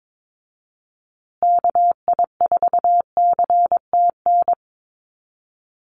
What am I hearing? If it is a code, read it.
Morse code, heard as a keyed tone, about 22 wpm, 700 Hz. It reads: KI4CTN
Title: USA Call Signs Repeated 2x 22wpm